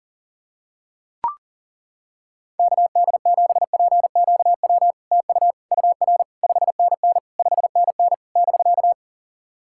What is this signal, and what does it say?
Morse code, heard as a keyed tone, about 42 wpm, 700 Hz. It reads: KD7PQW TU UR 5NN 5NN <BK>
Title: SOTA QSO Practice 40wpm